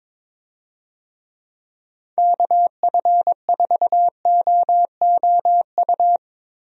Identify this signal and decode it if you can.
Morse code, heard as a keyed tone, about 22 wpm, 700 Hz. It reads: KF4OOU
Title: USA Call Signs - 6 Characters - Repeated 2x - 22wpm